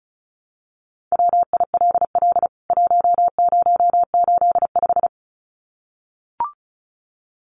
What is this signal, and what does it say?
Morse code, heard as a keyed tone, about 34 wpm, 700 Hz. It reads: WILL 1085
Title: CWT Exchange 35wpm